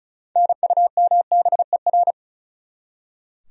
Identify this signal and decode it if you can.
Morse code, heard as a keyed tone, about 45 wpm, 700 Hz. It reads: NUMBER